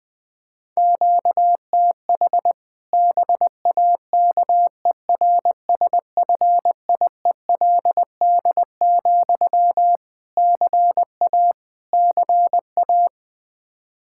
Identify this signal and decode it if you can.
Morse code, heard as a keyed tone, about 21 wpm, 700 Hz. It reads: QTH BAKERSFIELD, CA CA
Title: Practice QSO - Exchange 1 - Beginning - 20wpm